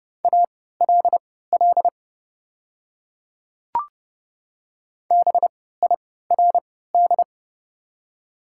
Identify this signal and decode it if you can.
Morse code, heard as a keyed tone, about 31 wpm, spaced as for 17 wpm, 700 Hz. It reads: ALL BIRD
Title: 3x Character Spacing - Top 500 Words - 30wpm